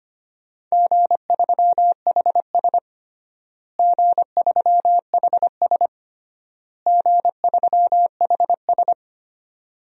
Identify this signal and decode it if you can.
Morse code, heard as a keyed tone, about 25 wpm, 700 Hz. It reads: G3HS G3HS G3HS
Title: DXCC-100 Call Signs - 4 Characters - Repeated 3x - 25wpm